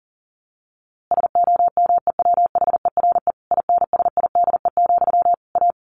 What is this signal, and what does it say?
Morse code, heard as a keyed tone, about 40 wpm, 700 Hz. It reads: SOMEWHERE INSIDE, A
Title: A Sound Of Thunder 40wpm